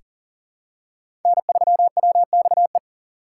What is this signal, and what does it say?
Morse code, heard as a keyed tone, about 40 wpm, 700 Hz. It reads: N3WXE
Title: USA Call Signs - 2x Character Spacing - 40wpm